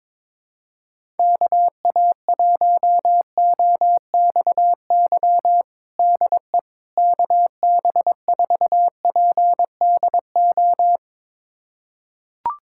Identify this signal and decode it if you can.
Morse code, heard as a keyed tone, about 22 wpm, 700 Hz. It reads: KA1OXY DE KB4PDO
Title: USA Callsign DE Callsign - 6 Characters - 22wpm